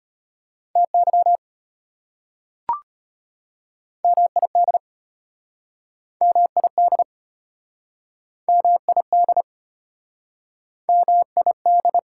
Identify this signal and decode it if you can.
Morse code, heard as a keyed tone, about 33 wpm, 700 Hz. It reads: TY MID MID MID MID